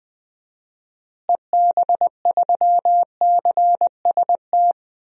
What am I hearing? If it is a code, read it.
Morse code, heard as a keyed tone, about 21 wpm, 700 Hz. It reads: EB3CST
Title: DXCC-100 Call Signs - 6 Characters - Repeated 2x - 20wpm